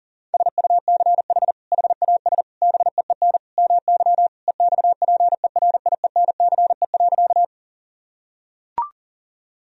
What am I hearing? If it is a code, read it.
Morse code, heard as a keyed tone, about 40 wpm, 700 Hz. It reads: SUCH HAS BEEN MY EXPERIENCE.
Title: Sentences from Top 1000 Words - Review Part 3 - 40wpm